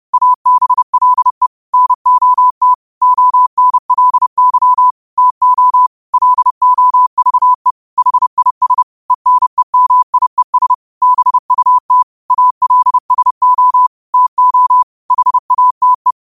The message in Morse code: ABLE NOT ONLY TO LOVE HIS ENEMIES BUT ALSO TO HATE

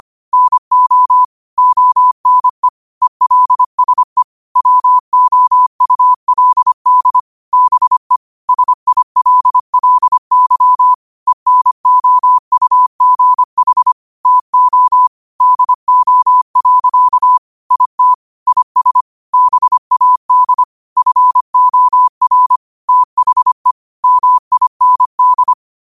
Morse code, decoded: NO ONE ELSE WOULD BE SILLY ENOUGH TO DO. IT IS BAD FOR THE MIND

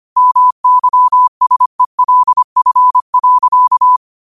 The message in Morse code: MYSELF.

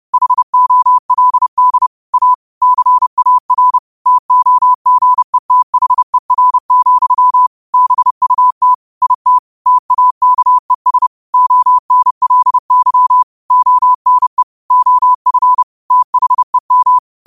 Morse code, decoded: HOLD A CAR TOGETHER, BUT IT TAKES ONLY ONE OF THEM

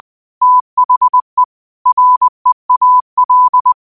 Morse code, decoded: THE REAL

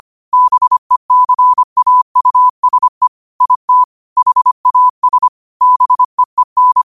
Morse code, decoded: BECAUSE IT HAS BEEN

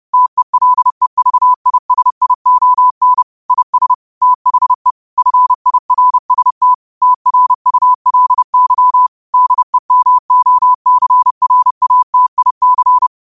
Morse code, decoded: TELEVISION IS THE FIRST TRULY DEMOCRATIC

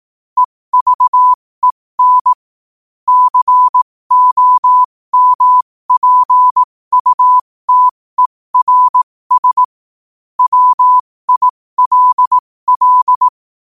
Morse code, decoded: EVEN COMPUTERS WILL

